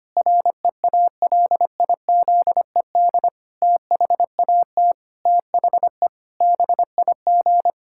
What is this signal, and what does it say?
Morse code, heard as a keyed tone, about 25 wpm, 700 Hz. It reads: REALIZED THAT THE BIG
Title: Pangram Challenge v2 - 25wpm